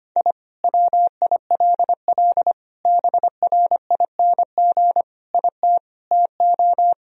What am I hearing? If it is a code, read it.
Morse code, heard as a keyed tone, about 25 wpm, 700 Hz. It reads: I WILL BRING IT TO
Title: Sentences from Top 1000 Words - Review Part 3 - 25wpm